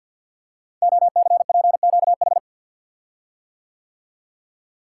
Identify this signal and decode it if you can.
Morse code, heard as a keyed tone, about 50 wpm, 700 Hz. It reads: OCPQH
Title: Code Groups - 5 Letter:Numbers - 50wpm